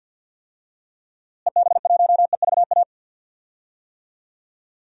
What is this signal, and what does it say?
Morse code, heard as a keyed tone, about 48 wpm, 700 Hz. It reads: EB1EVA
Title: DXCC-100 Call Signs - 6 Characters - Repeated 2x - 50wpm